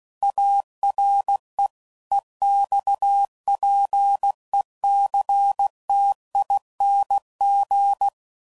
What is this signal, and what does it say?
Morse code, heard as a keyed tone, about 16 wpm, 800 Hz. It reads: ARE EXPECTING
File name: CW Code Groups @15WPM / QSOs / WA7QOX